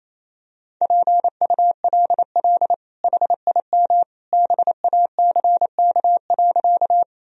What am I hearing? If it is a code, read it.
Morse code, heard as a keyed tone, about 28 wpm, 700 Hz. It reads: PULL HIM BACK.
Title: Sentences from Top 1000 Words - Review Part 2 - 28wpm